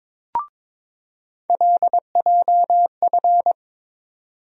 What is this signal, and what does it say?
Morse code, heard as a keyed tone, about 21 wpm, 700 Hz. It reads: LJF
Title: USA Call Sign Postfixes - 3 Letters 22wpm